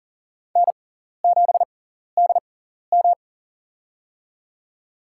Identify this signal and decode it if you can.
Morse code, heard as a keyed tone, about 40 wpm, spaced as for 14 wpm, 700 Hz. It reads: N7DM
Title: USA Call Signs - 6x Character Spacing - 40wpm